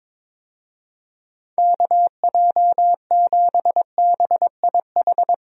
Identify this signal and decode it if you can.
Morse code, heard as a keyed tone, about 22 wpm, 700 Hz. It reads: KJ7BIH
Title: USA Call Signs - 6 Characters - Repeated 3x - 22wpm